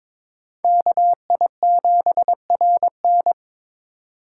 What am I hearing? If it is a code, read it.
Morse code, heard as a keyed tone, about 26 wpm, 700 Hz. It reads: KI7RN
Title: Speed Racing - USA Call Signs 22wpm